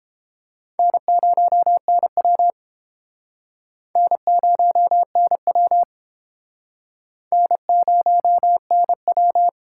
Morse code, heard as a keyed tone, about 30 wpm, 700 Hz. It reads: N0NW N0NW N0NW